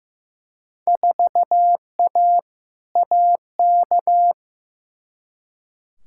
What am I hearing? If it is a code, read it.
Morse code, heard as a keyed tone, about 15 wpm, 700 Hz. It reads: 4A AK
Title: Field Day Exchanges Encore - 15wpm